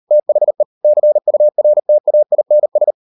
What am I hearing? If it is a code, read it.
Morse code, heard as a keyed tone, about 38 wpm, 600 Hz. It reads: THE CURTAINS